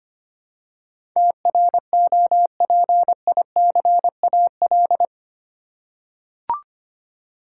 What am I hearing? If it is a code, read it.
Morse code, heard as a keyed tone, about 25 wpm, 700 Hz. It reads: TROPICAL
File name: 8 Letter Words - First 3 Letters Given - 25wpm